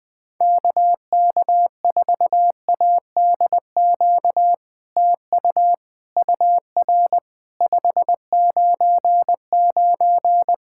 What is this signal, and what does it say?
Morse code, heard as a keyed tone, about 20 wpm, 700 Hz. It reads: KK4ADQ TU UR 599
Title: POTA QSO Practice - 20wpm